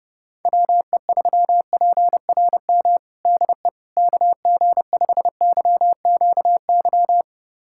Morse code, heard as a keyed tone, about 32 wpm, 700 Hz. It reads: WE3PRM DE KG5YQY
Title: USA Callsign DE Callsign - 6 Characters - 30wpm